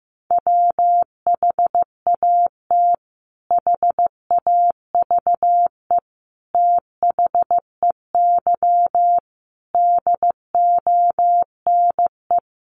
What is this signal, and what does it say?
Morse code, heard as a keyed tone, about 15 wpm, 700 Hz. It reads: WHAT HAVE THEY DONE
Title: Sentences from Top 500 Words 15wpm